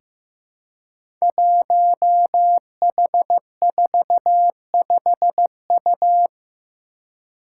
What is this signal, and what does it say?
Morse code, heard as a keyed tone, about 15 wpm, 700 Hz. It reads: 1H45U
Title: Five Challenging Letter-Number Combinations 15wpm